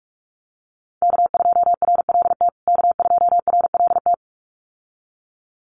Morse code, heard as a keyed tone, about 44 wpm, 700 Hz. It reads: K2RLT K2RLT
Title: USA Call Signs Repeated 2x 45wpm